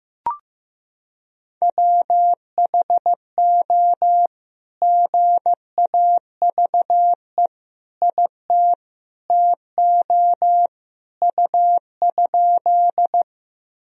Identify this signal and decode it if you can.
Morse code, heard as a keyed tone, about 16 wpm, 700 Hz. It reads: WHO GAVE IT TO U?
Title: Sentences from Top 500 Words and Abbreviations 15wpm